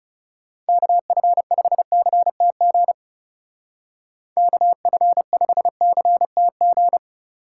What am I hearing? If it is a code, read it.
Morse code, heard as a keyed tone, about 37 wpm, 700 Hz. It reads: KF5CTG KF5CTG
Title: Speed Racing - USA 2x3 Call Signs 30wpm